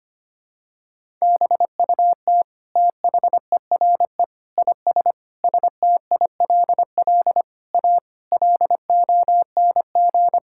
Morse code, heard as a keyed tone, about 25 wpm, 700 Hz. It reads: BUT THERE IS STILL A LONG
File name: Sets of 9 Words - 25wpm